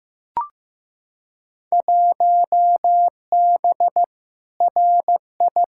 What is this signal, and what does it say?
Morse code, heard as a keyed tone, about 15 wpm, 700 Hz. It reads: 1B RI
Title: Field Day Exchanges Encore - 15wpm